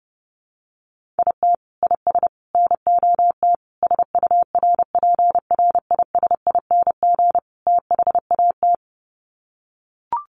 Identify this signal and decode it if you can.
Morse code, heard as a keyed tone, about 30 wpm, 700 Hz. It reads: IT IS NOT SURPRISING THAT
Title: Sets of 5 Words 30wpm